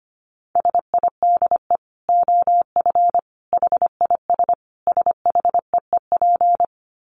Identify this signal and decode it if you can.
Morse code, heard as a keyed tone, about 25 wpm, 700 Hz. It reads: SIDE OF HIS SHEEP
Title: Dit Challenge 25wpm